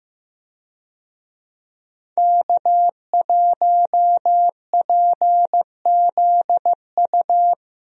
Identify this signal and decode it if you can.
Morse code, heard as a keyed tone, about 16 wpm, 700 Hz. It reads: K1PZU
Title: USA Call Signs - 5 Characters - Repeated 2x - 15wpm